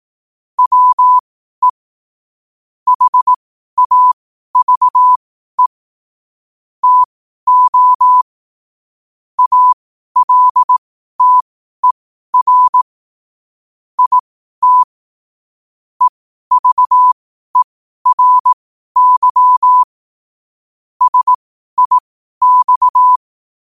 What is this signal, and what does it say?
Morse code, heard as a keyed tone, about 18 wpm, spaced as for 12 wpm, 1000 Hz. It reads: WE HAVE TO ALTER IT EVERY SIX
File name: qotdfiles / 12wpm